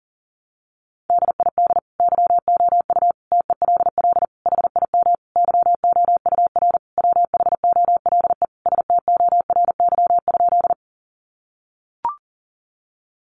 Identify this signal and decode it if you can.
Morse code, heard as a keyed tone, about 40 wpm, 700 Hz. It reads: DID YOU TELL HIM YOUR WHOLE STORY?
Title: Sentences from Top 400 Words 40wpm